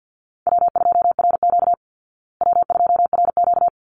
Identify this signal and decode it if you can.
Morse code, heard as a keyed tone, about 48 wpm, 700 Hz. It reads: W2RQ W2RQ
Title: Easy Call Signs Repeated 2x - 50wpm